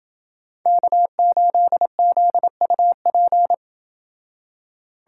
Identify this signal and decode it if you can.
Morse code, heard as a keyed tone, about 28 wpm, 700 Hz. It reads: K8ZUP